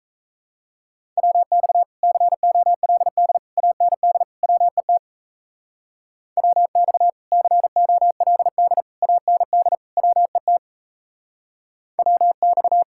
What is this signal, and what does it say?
Morse code, heard as a keyed tone, about 33 wpm, 700 Hz. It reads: WX COLD AND WET WX COLD AND WET WX
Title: Speed-Racing Common QSO Phrases Encore - 28wpm